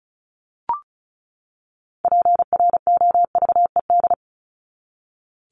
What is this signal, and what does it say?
Morse code, heard as a keyed tone, about 34 wpm, 700 Hz. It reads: PROVED
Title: ED Suffix - Top 100 Words with Base Words 35wpm